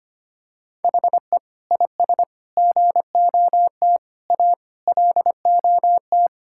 Morse code, heard as a keyed tone, about 25 wpm, 700 Hz. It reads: HE IS GOT A LOT
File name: practice / 25wpm / Sentences with 4-Word Phrases - 25wpm